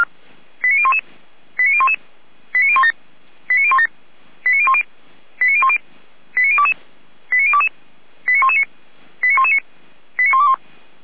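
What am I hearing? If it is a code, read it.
Five-tone selective call (ZVEI), heard as quick runs of five tones, tones of about 70 ms. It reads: 89011 89011 89017 89017 89010 89010 89022 89022 89119 89119 89121
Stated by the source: ZVEI five-tone selective call 89001, received on SDR.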